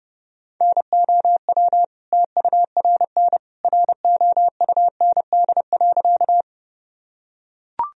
Morse code, heard as a keyed tone, about 30 wpm, 700 Hz. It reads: NOW TURN ROUND.
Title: Sentences from Top 1000 Words - Review Part 1 - 30wpm